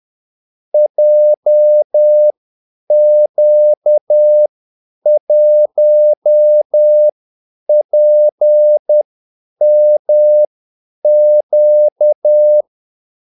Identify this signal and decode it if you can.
Morse code, heard as a keyed tone, about 10 wpm, spaced as for 8 wpm, 600 Hz. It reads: JQ1PMQ
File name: Callsigns-10-8wpm